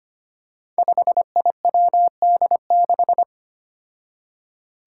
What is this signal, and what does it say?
Morse code, heard as a keyed tone, about 25 wpm, 700 Hz. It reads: 5IWD6
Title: Five Challenging Letter-Number Combinations 25wpm